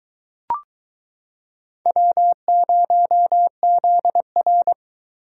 Morse code, heard as a keyed tone, about 23 wpm, 700 Hz. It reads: W0ZR